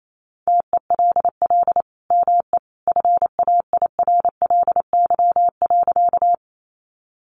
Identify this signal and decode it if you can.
Morse code, heard as a keyed tone, about 28 wpm, 700 Hz. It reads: TELL ME FAIRLY.
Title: Sentences from Top 1100 Words 28wpm